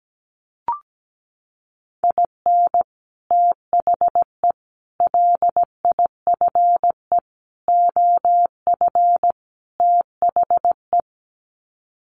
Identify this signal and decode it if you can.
Morse code, heard as a keyed tone, about 17 wpm, 700 Hz. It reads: IN THE LIFE OF THE
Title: Sets of 5 Words Encore2 17wpm